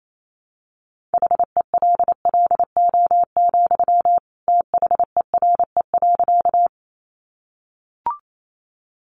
Morse code, heard as a keyed tone, about 28 wpm, 700 Hz. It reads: HELLO, THERE.
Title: Sentences from Top 1100 Words 28wpm